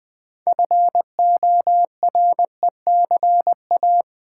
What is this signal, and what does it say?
Morse code, heard as a keyed tone, about 20 wpm, 700 Hz. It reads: FORECA
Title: Build A Word - 2 Letters At A Time - 8-Letter Words - 20wpm